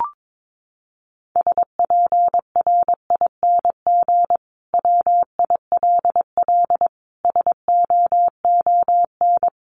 Morse code, heard as a keyed tone, about 22 wpm, 700 Hz. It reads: SPRING WILL SOON
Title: Sentences from Top 1200 Words 22wpm